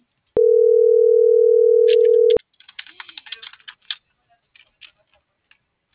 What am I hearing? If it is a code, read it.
Ringback tone (North America), 440 + 480 Hz.